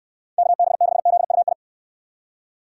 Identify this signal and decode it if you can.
Morse code, heard as a keyed tone, about 69 wpm, 700 Hz. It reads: KB6ZUI